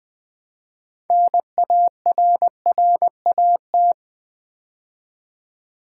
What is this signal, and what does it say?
Morse code, heard as a keyed tone, about 20 wpm, 700 Hz. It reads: NARRAT
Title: Build A Word - 3 Letters At A Time - 9-Letter Words - 20wpm